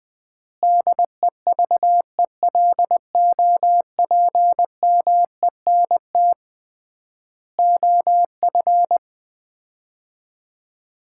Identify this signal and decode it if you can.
Morse code, heard as a keyed tone, about 20 wpm, 700 Hz. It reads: DEVELOPMENT OF
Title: 3x Word-Spacing - Sets of 3 Words - 20wpm